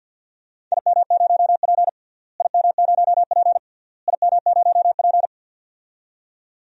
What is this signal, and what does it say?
Morse code, heard as a keyed tone, about 53 wpm, 700 Hz. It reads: IM0P IM0P IM0P
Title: DXCC-100 Call Signs - 4 Characters - Repeated 3x - 50wpm